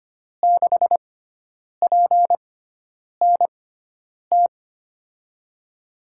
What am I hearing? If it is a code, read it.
Morse code, heard as a keyed tone, about 25 wpm, spaced as for 9 wpm, 700 Hz. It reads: 6PNT
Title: USA Call Signs - 6 Characters - 6x Character Spacing - 25wpm